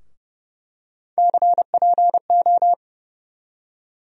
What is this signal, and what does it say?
Morse code, heard as a keyed tone, about 30 wpm, 700 Hz. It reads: CPO
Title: USA Call Sign Postfixes - 3 Letters 30wpm